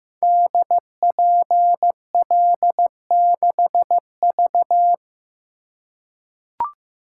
Morse code, heard as a keyed tone, about 15 wpm, 700 Hz. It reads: DPL6V
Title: Code Groups - 5 Letter:Numbers - 15wpm